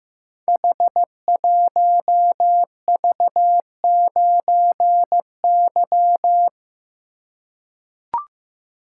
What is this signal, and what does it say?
Morse code, heard as a keyed tone, about 15 wpm, 700 Hz. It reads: H1V9Y
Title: Code Groups - 5 Letter:Numbers - 15wpm